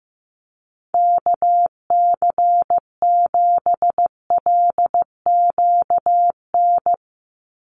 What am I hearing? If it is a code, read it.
Morse code, heard as a keyed tone, about 15 wpm, 700 Hz. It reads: KC7LQN
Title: USA Call Signs 15wpm